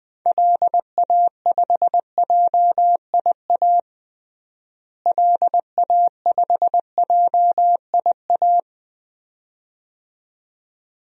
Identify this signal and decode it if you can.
Morse code, heard as a keyed tone, about 20 wpm, 700 Hz. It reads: LA5JIA LA5JIA
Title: DXCC-100 Call Signs - 6 Characters - Repeated 2x - 20wpm